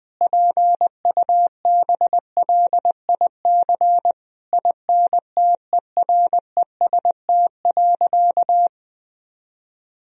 Morse code, heard as a keyed tone, about 20 wpm, 700 Hz. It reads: PUBLIC INTEREST.
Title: Sentences from Top 1000 Words - Review Part 4 - 20wpm